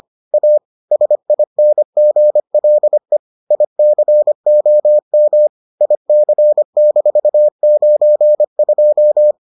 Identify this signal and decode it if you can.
Morse code, heard as a keyed tone, about 25 wpm, 600 Hz. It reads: A SINGLE ICOM IC-92